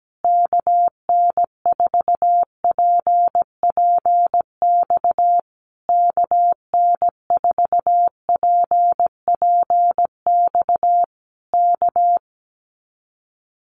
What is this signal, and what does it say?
Morse code, heard as a keyed tone, about 17 wpm, 700 Hz. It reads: KN4PPX KN4PPX K
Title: Common QSO Phrases Encore 17wpm